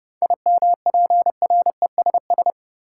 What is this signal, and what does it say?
Morse code, heard as a keyed tone, about 30 wpm, 700 Hz. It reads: IMPRESS
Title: Seven Letter Words - 30wpm